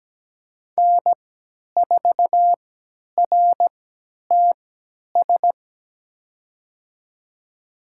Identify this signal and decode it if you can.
Morse code, heard as a keyed tone, about 17 wpm, spaced as for 10 wpm, 700 Hz. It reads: N4RTS
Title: USA Call Signs - 5 Characters - 3x Character Spacing - 17wpm